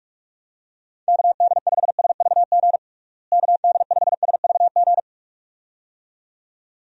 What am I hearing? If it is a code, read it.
Morse code, heard as a keyed tone, about 47 wpm, 700 Hz. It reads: KD5SVG KD5SVG
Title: USA Call Signs - 6 Characters - Repeated 2x - 45wpm